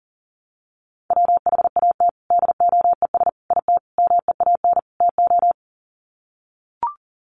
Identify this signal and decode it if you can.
Morse code, heard as a keyed tone, about 40 wpm, 700 Hz. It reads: WHAT DOES IT MEAN TO
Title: Sets of 5 Words Encore2 40wpm